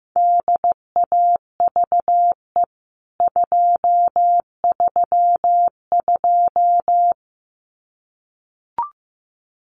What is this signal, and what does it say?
Morse code, heard as a keyed tone, about 15 wpm, 700 Hz. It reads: DAVE 232
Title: CWT Exchange 15wpm